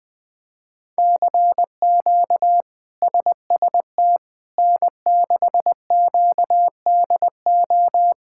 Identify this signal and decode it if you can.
Morse code, heard as a keyed tone, about 20 wpm, 700 Hz. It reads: CQ SST N6QDO
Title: K1USN SST QSO Practice - 20wpm